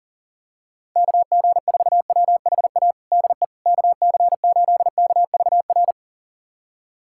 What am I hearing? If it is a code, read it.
Morse code, heard as a keyed tone, about 40 wpm, 700 Hz. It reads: KG4WHA DE KC8KVR
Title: USA Callsign DE Callsign - 6 Characters - 40wpm